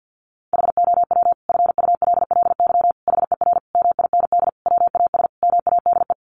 Morse code, heard as a keyed tone, about 50 wpm, 700 Hz. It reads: HOW FULLY HER MIND WAS MADE
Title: Sentences from Top 1300 Words 50wpm